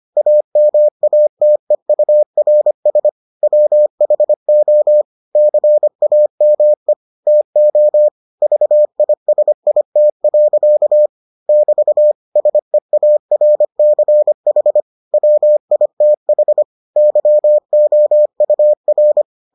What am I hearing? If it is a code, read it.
Morse code, heard as a keyed tone, about 25 wpm, 600 Hz. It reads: AMATEURS WHO CAME TO VISIT. <BT> SEARCH WITH YOUR